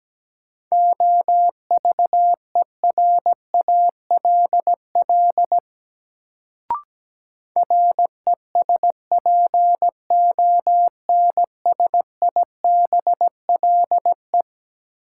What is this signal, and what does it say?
Morse code, heard as a keyed tone, about 17 wpm, 700 Hz. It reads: OVERALL RESPONSIBLE
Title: Sentences from Top 1000 Words - Review Part 7 - New Word Drill - 17wpm